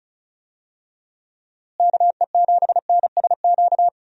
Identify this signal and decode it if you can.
Morse code, heard as a keyed tone, about 36 wpm, 700 Hz. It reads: KE7NSQ
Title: USA Call Signs - 6 Characters - Repeated 3x - 35wpm